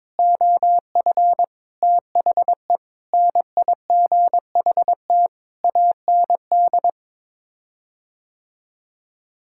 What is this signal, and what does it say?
Morse code, heard as a keyed tone, about 22 wpm, 700 Hz. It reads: OF THE NIGHT AND
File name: practice / 22wpm / Sets of 7 Words - 22wpm